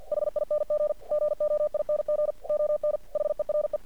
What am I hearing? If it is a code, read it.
Morse code, heard as a keyed tone, about 50 wpm, 600 Hz. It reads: HING GOING ON HERE
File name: archived / CRSnet